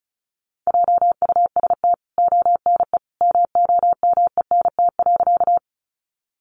Morse code, heard as a keyed tone, about 34 wpm, 700 Hz. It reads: JUST ONE MOMENT.
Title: Sentences from Top 1000 Words 35wpm